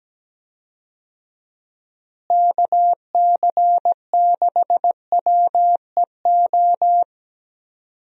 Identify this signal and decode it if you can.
Morse code, heard as a keyed tone, about 17 wpm, 700 Hz. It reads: KC6WEO
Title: USA Call Signs - 6 Characters - Repeated 2x - 17wpm